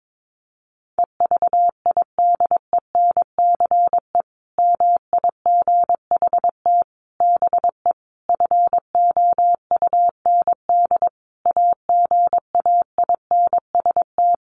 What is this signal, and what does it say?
Morse code, heard as a keyed tone, about 22 wpm, 700 Hz. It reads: EVIDENCE MIGHT BE FOUND AGAINST
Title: Sentences from Top 1300 Words 22wpm